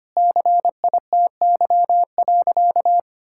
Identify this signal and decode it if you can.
Morse code, heard as a keyed tone, about 25 wpm, 700 Hz. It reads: CITY.
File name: practice / 25wpm / Sentences with 2-Word Phrases - 25wpm